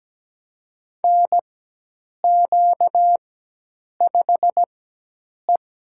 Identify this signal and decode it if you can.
Morse code, heard as a keyed tone, about 17 wpm, spaced as for 8 wpm, 700 Hz. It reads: NQ5E
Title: USA Call Signs - 4 Characters - 4x Character Spacing - 17wpm